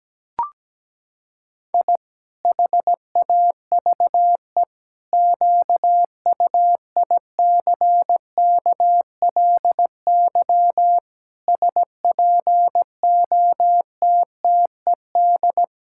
Morse code, heard as a keyed tone, about 17 wpm, 700 Hz. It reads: I HAVE QUICKLY SPOTTED
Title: Pangram Challenge v2 - 17wpm